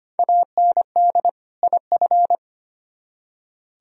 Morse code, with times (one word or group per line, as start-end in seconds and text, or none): 0.2-1.3 AND
1.6-2.4 IF